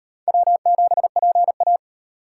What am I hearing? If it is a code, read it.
Morse code, heard as a keyed tone, about 34 wpm, 700 Hz. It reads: W7PA